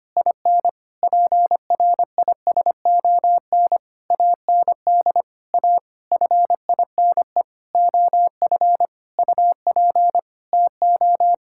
Morse code, heard as a keyed tone, about 25 wpm, 700 Hz. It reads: IN PRISON AND A FINE OF UP TO
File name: practice / 25wpm / Sentences with 9-Word Phrases - 25wpm